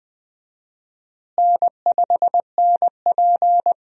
Morse code, heard as a keyed tone, about 20 wpm, 700 Hz. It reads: N5NP
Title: USA Call Signs - 4 Characters - Repeated 3x - 20wpm